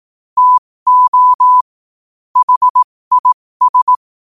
Morse code, heard as a keyed tone, about 18 wpm, spaced as for 15 wpm, 1000 Hz. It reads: TO HIS
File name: qotdfiles / 15wpm